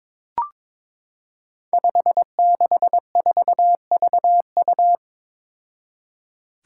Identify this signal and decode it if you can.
Morse code, heard as a keyed tone, about 22 wpm, 700 Hz. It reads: 564VU
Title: Five Challenging Letter-Number Combinations 22wpm